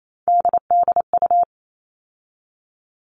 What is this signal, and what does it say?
Morse code, heard as a keyed tone, about 28 wpm, 700 Hz. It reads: DDU
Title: All Cut Numbers - 3-Digit 28wpm